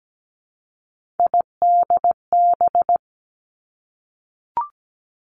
Morse code, heard as a keyed tone, about 17 wpm, 700 Hz. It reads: IDB
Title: Three Challenging Letter-Number Combinations 17wpm